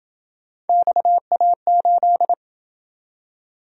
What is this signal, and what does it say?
Morse code, heard as a keyed tone, about 26 wpm, 700 Hz. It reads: XA8